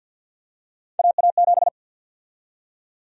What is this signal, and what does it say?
Morse code, heard as a keyed tone, about 52 wpm, 700 Hz. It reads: AA7
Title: USA Call Sign Prefixes - 50wpm